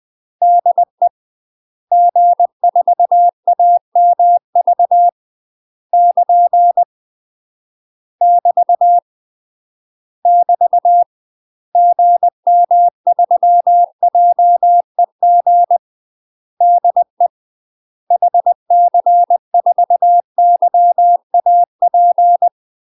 Morse code, heard as a keyed tone, about 20 wpm, 700 Hz. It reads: DE G4AMV <KN> <BT> <BT> GM3JEG DE HC4YAP
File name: UK MASTER QSOs TRAINING / 20WPM / UK MASTER QSOs 0060